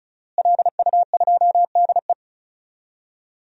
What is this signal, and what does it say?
Morse code, heard as a keyed tone, about 36 wpm, 700 Hz. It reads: LU2DE
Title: DXCC-100 Call Signs - 5 Characters - Repeated 3x - 35wpm